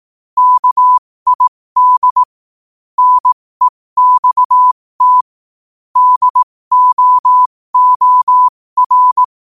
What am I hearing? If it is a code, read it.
Morse code, heard as a keyed tone, about 18 wpm, spaced as for 15 wpm, 1000 Hz. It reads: KID NEXT DOOR
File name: qotdfiles / 15wpm